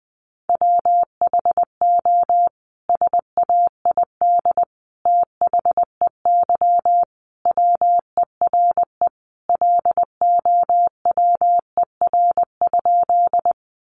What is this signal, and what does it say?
Morse code, heard as a keyed tone, about 20 wpm, 700 Hz. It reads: WHO SAID THEY WERE LOWER?
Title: Sentences from Top 700 Words 20wpm